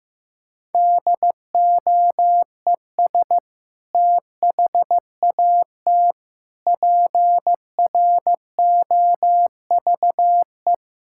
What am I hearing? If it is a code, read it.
Morse code, heard as a keyed tone, about 15 wpm, 700 Hz. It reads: DOES THAT PROVE
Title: Sentences from Top 1600 Words 15wpm